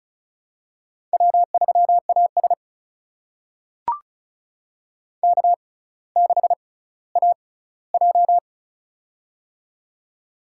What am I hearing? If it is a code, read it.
Morse code, heard as a keyed tone, about 36 wpm, spaced as for 18 wpm, 700 Hz. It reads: W3AS K6AJ
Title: USA Call Signs - 4 Characters - 6x Character Spacing - 35wpm